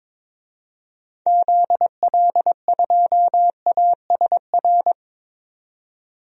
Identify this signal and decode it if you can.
Morse code, heard as a keyed tone, about 22 wpm, 700 Hz. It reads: ZL2ASR
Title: DXCC-100 Call Signs - 6 Characters - 22wpm